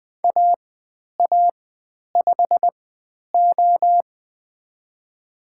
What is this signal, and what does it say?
Morse code, heard as a keyed tone, about 20 wpm, spaced as for 10 wpm, 700 Hz. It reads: AA5O
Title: 095 Lesson 16g - Letter c - Call Signs 20wpm